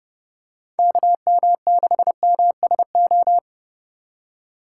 Morse code, heard as a keyed tone, about 30 wpm, 700 Hz. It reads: KM6MSO
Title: USA Call Signs - 6 Characters - 3x Character Spacing - 30wpm